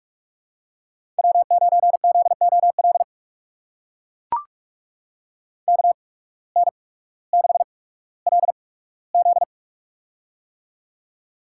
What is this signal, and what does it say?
Morse code, heard as a keyed tone, about 44 wpm, spaced as for 19 wpm, 700 Hz. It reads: W9ZOL KN6LZ
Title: USA Call Signs - 5 Characters - 8x Character Spacing - 45wpm